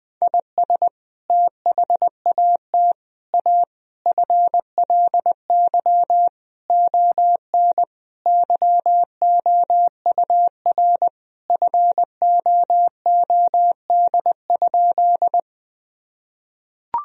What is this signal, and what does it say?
Morse code, heard as a keyed tone, about 21 wpm, 700 Hz. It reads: IS THAT A FLY ON YOUR FOOD?
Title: Sentences from Top 1000 Words - Review Part 2 - 20wpm